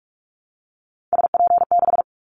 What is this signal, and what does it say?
Morse code, heard as a keyed tone, about 46 wpm, 700 Hz. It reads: SP6
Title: Callsign Prefixes for Top 25 DXCC - 45wpm